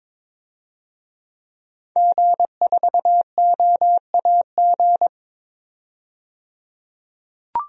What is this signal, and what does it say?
Morse code, heard as a keyed tone, about 22 wpm, 700 Hz. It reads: G4OAG
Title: DXCC-100 Call Signs - 5 Characters - 22wpm